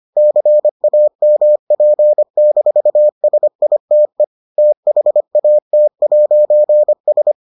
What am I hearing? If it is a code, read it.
Morse code, heard as a keyed tone, about 25 wpm, 600 Hz. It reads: CAMP-SITE THAT'S